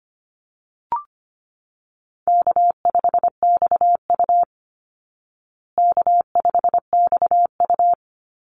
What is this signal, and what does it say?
Morse code, heard as a keyed tone, about 25 wpm, 700 Hz. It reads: K5XU K5XU
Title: Easy Call Signs Repeated 2x - 25wpm